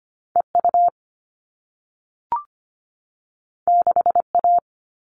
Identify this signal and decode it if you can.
Morse code, heard as a keyed tone, about 25 wpm, 700 Hz. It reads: EU 6A
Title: All Cut Numbers - 2-Digit 25wpm